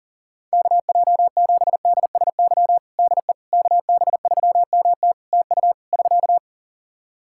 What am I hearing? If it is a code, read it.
Morse code, heard as a keyed tone, about 37 wpm, 700 Hz. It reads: KJ7DSY DE KB3MT TU <SK>
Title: K1USN SST QSO Practice 40wpm